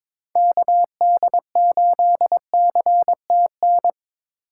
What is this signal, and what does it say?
Morse code, heard as a keyed tone, about 23 wpm, 700 Hz. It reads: KD8CTN
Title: USA Call Signs - 6 Characters - 8x Character Spacing - 22wpm